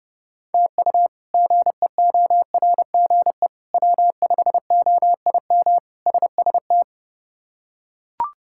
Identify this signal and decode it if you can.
Morse code, heard as a keyed tone, about 30 wpm, 700 Hz. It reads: TU GEORGE W5OIM SST
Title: K1USN SST QSO Practice - 30wpm